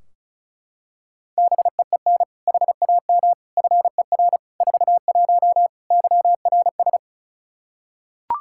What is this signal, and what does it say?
Morse code, heard as a keyed tone, about 34 wpm, 700 Hz. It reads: BEEN HAM FER 41 YRS
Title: Practice QSO - Exchange 3 - More Info - 35wpm